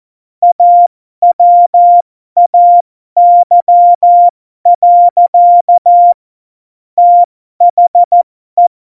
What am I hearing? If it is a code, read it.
Morse code, heard as a keyed tone, about 14 wpm, spaced as for 12 wpm, 700 Hz. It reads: AWAY. THE
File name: Morsekurs / Witze